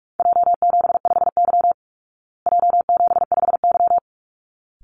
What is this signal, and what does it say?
Morse code, heard as a keyed tone, about 44 wpm, 700 Hz. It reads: J75Y J75Y
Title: Call Signs Repeated 3x - 45wpm